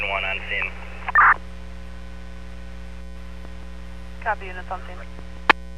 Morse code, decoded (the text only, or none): none